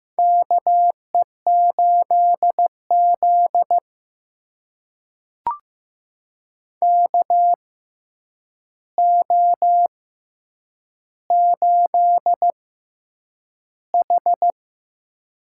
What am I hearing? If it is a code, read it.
Morse code, heard as a keyed tone, about 15 wpm, spaced as for 8 wpm, 700 Hz. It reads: KE8Z KO8H
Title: USA Call Signs - 4 Characters - 6x Character Spacing - 15wpm